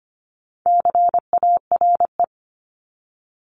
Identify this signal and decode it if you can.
Morse code, heard as a keyed tone, about 25 wpm, 700 Hz. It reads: CARE
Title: Multi-letter Phonemes 25wpm